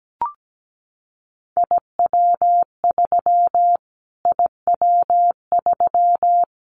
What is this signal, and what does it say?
Morse code, heard as a keyed tone, about 17 wpm, 700 Hz. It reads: IW3 IW3
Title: Callsign Prefixes for Top 25 DXCC - Repeated 3x - 17wpm